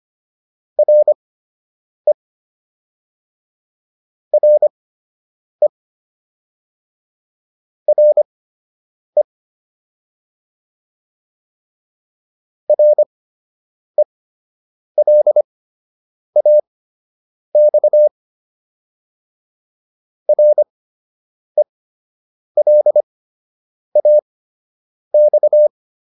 Morse code, heard as a keyed tone, about 25 wpm, spaced as for 8 wpm, 600 Hz. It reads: RE RE RE RELAX RELAX